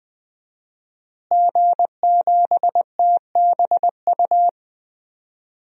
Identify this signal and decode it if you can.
Morse code, heard as a keyed tone, about 20 wpm, 700 Hz. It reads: G7TBU
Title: DXCC-100 Call Signs - 5 Characters - Repeated 3x - 20wpm